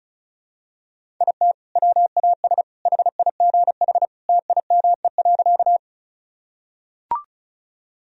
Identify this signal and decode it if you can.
Morse code, heard as a keyed tone, about 36 wpm, 690 Hz. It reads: IT WAS HIGH TIME.
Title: Sentences from Top 1000 Words - Review Part 1 - 35wpm